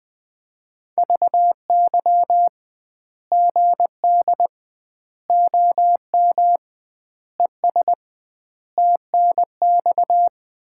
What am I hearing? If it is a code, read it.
Morse code, heard as a keyed tone, about 21 wpm, 700 Hz. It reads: VY GD OM ES TNX
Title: Common QSO Phrases - 2x Word-Spacing - 20wpm